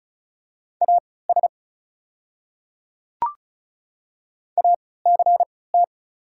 Morse code, heard as a keyed tone, about 34 wpm, spaced as for 20 wpm, 690 Hz. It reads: AS ACT
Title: 3x Character Spacing - Top 500 Words - 35wpm